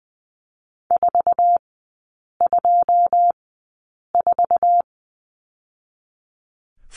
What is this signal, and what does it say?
Morse code, heard as a keyed tone, about 20 wpm, spaced as for 8 wpm, 700 Hz. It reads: 424
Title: 178 Lesson 29e - Slash - Three Characters 20wpm